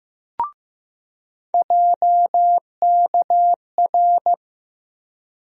Morse code, heard as a keyed tone, about 15 wpm, 700 Hz. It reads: JKR
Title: USA Call Sign Postfixes - 3 Letters 15wpm